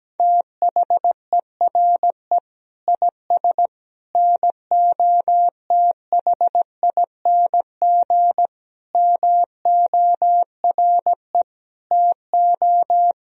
Morse code, heard as a keyed tone, about 17 wpm, 700 Hz. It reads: THERE IS NOTHING MORE TO
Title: Sentences from Top 500 Words and Abbreviations 17wpm